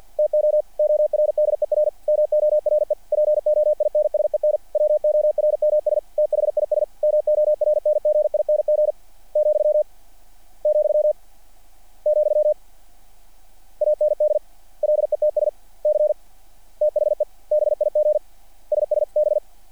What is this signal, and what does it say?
Morse code, heard as a keyed tone, about 48 wpm, 600 Hz. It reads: TO ORDER MORE POINSEN WORMS THIS MORNGING , , , AND LETS C THE BIG SS B
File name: archived / CRSnet